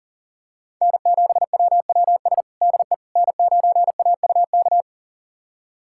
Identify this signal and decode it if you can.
Morse code, heard as a keyed tone, about 40 wpm, 700 Hz. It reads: N7WWS DE N9AUK
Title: USA Callsign DE Callsign - 5 Characters - 40wpm